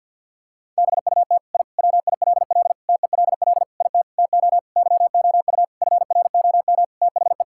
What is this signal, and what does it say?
Morse code, heard as a keyed tone, about 51 wpm, 700 Hz. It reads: BUT I WILL TELL IT TO YOU FROM THE
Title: Sentences from Top 1000 Words - Review Part 5 - 50wpm